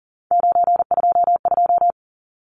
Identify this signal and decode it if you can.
Morse code, heard as a keyed tone, about 37 wpm, 700 Hz. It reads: 922
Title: Numbers - 3-Digit 40wpm